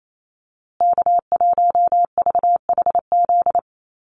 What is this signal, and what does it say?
Morse code, heard as a keyed tone, about 28 wpm, 700 Hz. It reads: K1VHZ
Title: USA Call Signs Repeated 2x 28wpm